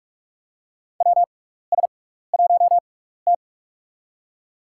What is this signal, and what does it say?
Morse code, heard as a keyed tone, about 43 wpm, spaced as for 15 wpm, 700 Hz. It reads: WS1T
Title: USA Call Signs - 4 Characters - 6x Character Spacing - 45wpm